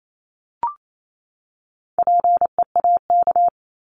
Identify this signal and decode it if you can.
Morse code, heard as a keyed tone, about 28 wpm, 700 Hz. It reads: PEAK
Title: Top 1751-2000 Words 28wpm